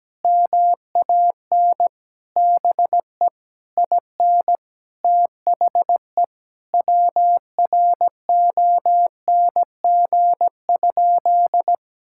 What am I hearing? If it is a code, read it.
Morse code, heard as a keyed tone, about 17 wpm, 700 Hz. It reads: MAN BE IN THE WRONG?
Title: Sentences from Top 1000 Words - Review Part 6 - 17wpm